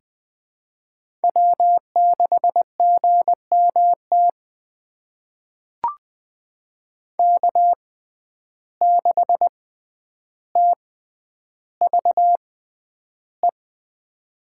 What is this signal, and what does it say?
Morse code, heard as a keyed tone, about 20 wpm, spaced as for 10 wpm, 700 Hz. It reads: W6GMT K6TVE
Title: USA Call Signs - 5 Characters - 6x Character Spacing - 20wpm